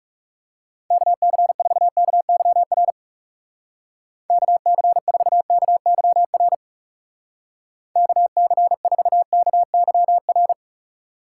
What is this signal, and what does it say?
Morse code, heard as a keyed tone, about 36 wpm, 700 Hz. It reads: KC4KYR KC4KYR KC4KYR
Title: Speed Racing - USA 2x3 Call Signs 30wpm